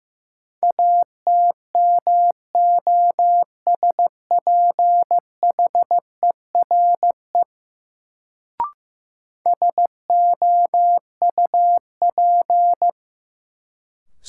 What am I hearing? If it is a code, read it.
Morse code, heard as a keyed tone, about 15 wpm, 700 Hz. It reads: ATMOSPHERE SOUP
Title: Sentences from Top 1600 Words - New Word Drill Alt - 15wpm